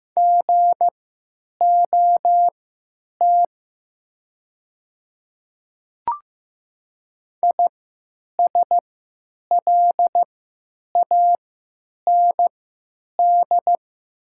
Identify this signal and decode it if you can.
Morse code, heard as a keyed tone, about 15 wpm, spaced as for 9 wpm, 700 Hz. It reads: GOT ISLAND
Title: 3x Character Spacing - Top 500 Words - 15wpm